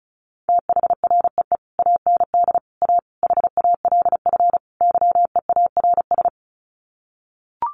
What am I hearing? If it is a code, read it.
Morse code, heard as a keyed tone, about 34 wpm, 700 Hz. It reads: THREE AND A HALF YEARS
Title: Sets of 5 Words Encore2 35wpm